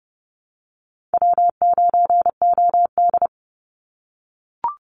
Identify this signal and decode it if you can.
Morse code, heard as a keyed tone, about 30 wpm, 700 Hz. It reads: W9OD
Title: Easy Call Signs - 30wpm